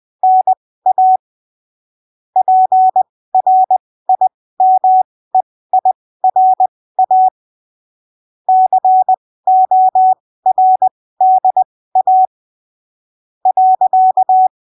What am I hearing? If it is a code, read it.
Morse code, heard as a keyed tone, about 20 wpm, spaced as for 15 wpm, 750 Hz. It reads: NA PRIMEIRA CORDA .
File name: arquivos / texto / texto 29 20ppm